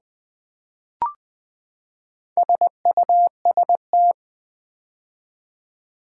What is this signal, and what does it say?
Morse code, heard as a keyed tone, about 20 wpm, 700 Hz. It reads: SUST
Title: Build A Word - 4 Letters At A Time - 11-Letter Words - 20wpm